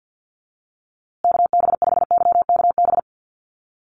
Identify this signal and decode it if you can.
Morse code, heard as a keyed tone, about 50 wpm, 700 Hz. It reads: KB5YKB
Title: USA Call Signs Repeated 2x 50wpm